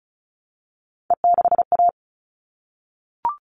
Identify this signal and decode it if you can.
Morse code, heard as a keyed tone, about 34 wpm, 700 Hz. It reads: E6A
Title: All Cut Numbers - 3-Digit 35wpm